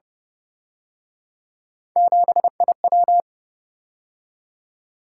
Morse code, heard as a keyed tone, about 30 wpm, 700 Hz. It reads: 7IW
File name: Speed-Racing - Three Letter-Number Combinations - 25wpm